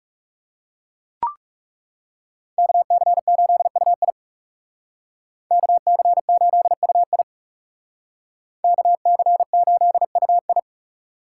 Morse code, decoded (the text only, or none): KC8UI KC8UI KC8UI